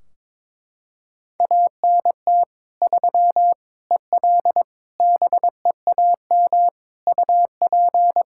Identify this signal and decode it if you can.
Morse code, heard as a keyed tone, about 22 wpm, 700 Hz. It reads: ANT 3 EL BEAM UP
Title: Practice QSO - Exchange 2 - Info - 22wpm